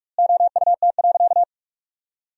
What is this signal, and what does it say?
Morse code, heard as a keyed tone, about 47 wpm, 700 Hz. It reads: OUT.
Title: 2x Word-Spacing - Sentences from Top 500 Words - 45wpm